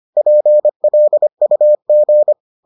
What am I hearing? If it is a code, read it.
Morse code, heard as a keyed tone, about 25 wpm, 600 Hz. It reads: PLUG